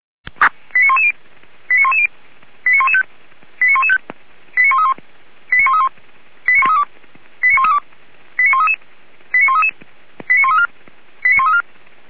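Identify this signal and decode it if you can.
Five-tone selective call (ZVEI), heard as quick runs of five tones, tones of about 70 ms. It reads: 89110 89110 89115 89115 89131 89131 89132 89132 89133 89133 89134 89134
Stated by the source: Five-tone ZVEI selective call received on SDR.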